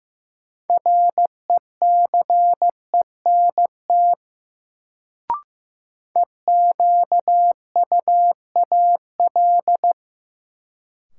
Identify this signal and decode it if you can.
Morse code, heard as a keyed tone, about 15 wpm, 700 Hz. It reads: RECENT EQUAL
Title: Sentences from Top 1400 Words - New Word Drill Alt - 15wpm